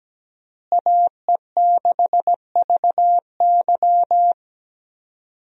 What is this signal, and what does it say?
Morse code, heard as a keyed tone, about 17 wpm, 700 Hz. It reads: AE6VY
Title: USA Call Signs - 5 Characters - Repeated 3x - 17wpm